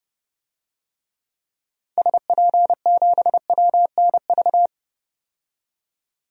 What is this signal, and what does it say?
Morse code, heard as a keyed tone, about 30 wpm, 700 Hz. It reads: SP7WNV
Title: DXCC-100 Call Signs - 6 Characters - 30wpm